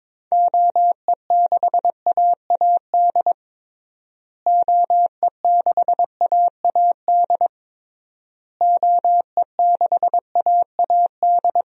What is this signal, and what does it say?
Morse code, heard as a keyed tone, about 22 wpm, 700 Hz. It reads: OE6AAD OE6AAD OE6AAD
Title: DXCC-100 Call Signs - 6 Characters - Repeated 3x - 22wpm